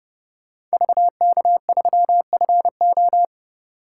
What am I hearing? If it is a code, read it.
Morse code, heard as a keyed tone, about 34 wpm, 700 Hz. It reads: VK3FO